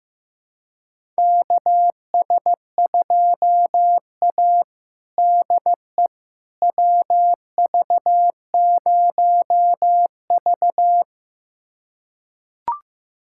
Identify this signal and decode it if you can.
Morse code, heard as a keyed tone, about 15 wpm, 700 Hz. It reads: KS2A DE WV0V
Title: USA Callsign DE Callsign - 4 Characters - 15wpm